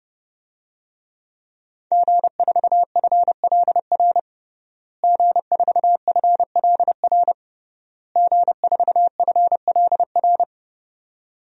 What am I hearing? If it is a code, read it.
Morse code, heard as a keyed tone, about 32 wpm, 700 Hz. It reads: G4FLR G4FLR G4FLR
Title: DXCC-100 Call Signs - 5 Characters - Repeated 3x - 30wpm